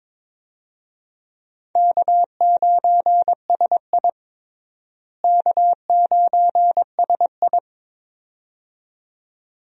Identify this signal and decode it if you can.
Morse code, heard as a keyed tone, about 22 wpm, 700 Hz. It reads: K9SI K9SI
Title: USA Call Signs - 4 Characters - Repeated 2x - 22wpm